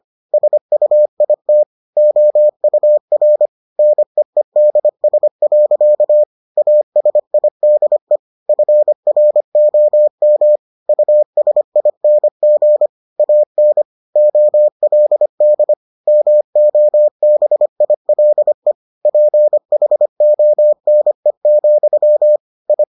Morse code, heard as a keyed tone, about 25 wpm, 600 Hz. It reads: SUIT OUR NEEDS. ASIDE FROM USING AN OLD MOBILE PHONE, I